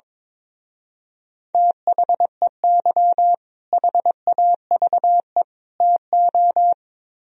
Morse code, heard as a keyed tone, about 22 wpm, 700 Hz. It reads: THEY HAVE TO
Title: Speed-Racing - Sets of 3 Words 22wpm